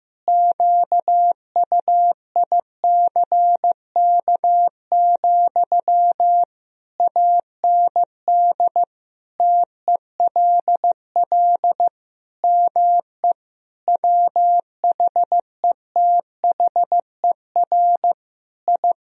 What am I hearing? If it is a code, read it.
Morse code, heard as a keyed tone, about 15 wpm, 700 Hz. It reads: QUICK, AND TELL ME WHETHER I
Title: Sentences from Top 1000 Words - Review Part 3 - 15wpm